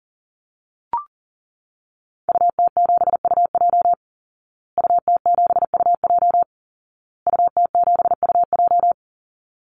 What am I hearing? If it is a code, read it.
Morse code, heard as a keyed tone, about 40 wpm, 700 Hz. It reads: UT7UJ UT7UJ UT7UJ
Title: Call Signs Repeated 2x - 40wpm